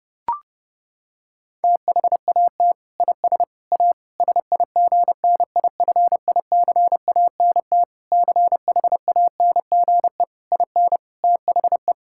Morse code, heard as a keyed tone, about 32 wpm, 700 Hz. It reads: THAT IS A SIGNIFICANT CHANGE IN THE
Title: Sentences from Top 1000 Words - Review Part 6 - 30wpm